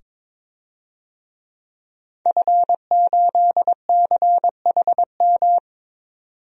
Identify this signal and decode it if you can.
Morse code, heard as a keyed tone, about 22 wpm, 700 Hz. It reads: F8CHM
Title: DXCC-100 Call Signs - 5 Characters - 22wpm